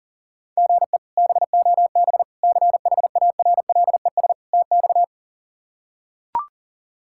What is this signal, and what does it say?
Morse code, heard as a keyed tone, about 40 wpm, 700 Hz. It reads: GE BOB CHARLES TX
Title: K1USN SST QSO Practice - 40wpm